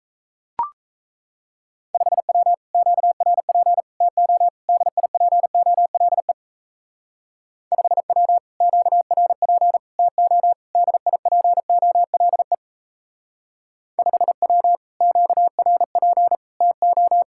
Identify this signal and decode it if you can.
Morse code, heard as a keyed tone, about 34 wpm, 700 Hz. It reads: 5W QRP TO DIPOLE 5W QRP TO DIPOLE 5W QRP TO
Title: Speed-Racing Common QSO Phrases - 28wpm